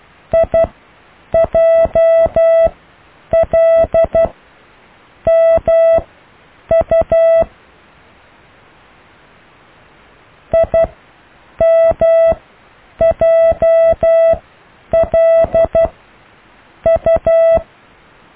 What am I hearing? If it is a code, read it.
Morse code, heard as a keyed tone, about 12 wpm, spaced as for 7 wpm, 660 Hz. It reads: IJLMU IMJLU
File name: cw / IJLMU